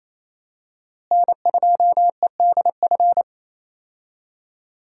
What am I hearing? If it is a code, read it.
Morse code, heard as a keyed tone, about 27 wpm, 700 Hz. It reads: N2EDF
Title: USA Call Signs - 5 Characters - Repeated 3x - 28wpm